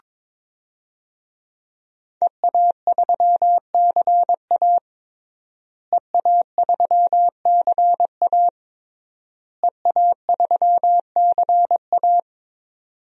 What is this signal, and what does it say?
Morse code, heard as a keyed tone, about 22 wpm, 700 Hz. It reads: EA3CA EA3CA EA3CA
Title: DXCC-100 Call Signs - 5 Characters - Repeated 3x - 22wpm